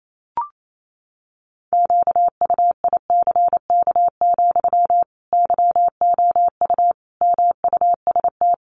Morse code, heard as a keyed tone, about 28 wpm, 700 Hz. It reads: QUICK, YOU MUST
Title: Sentences from Top 500 Words 28wpm